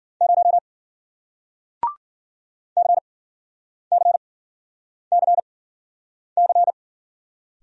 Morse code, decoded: 0 CCCC

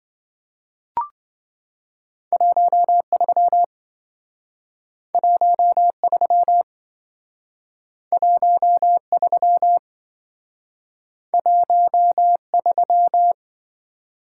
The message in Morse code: 13 13 13 13

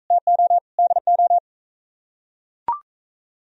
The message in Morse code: TO DO

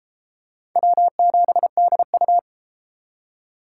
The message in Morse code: W7DU

